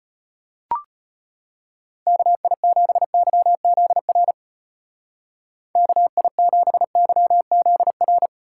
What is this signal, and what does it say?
Morse code, heard as a keyed tone, about 30 wpm, 700 Hz. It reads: KI7YZR KI7YZR
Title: Speed Racing - USA 2x3 Call Signs 25wpm